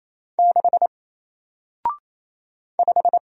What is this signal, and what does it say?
Morse code, heard as a keyed tone, about 27 wpm, 700 Hz. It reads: 6 5
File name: Single-Letter-Number-28wpm